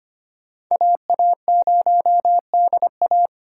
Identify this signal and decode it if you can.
Morse code, heard as a keyed tone, about 25 wpm, 700 Hz. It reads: AA0DA
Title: USA Call Signs - 5 Characters - 4x Character Spacing - 25wpm